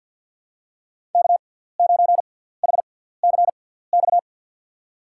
Morse code, heard as a keyed tone, about 53 wpm, spaced as for 17 wpm, 700 Hz. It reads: K9HCX
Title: USA Call Signs - 5 Characters - 6x Character Spacing - 50wpm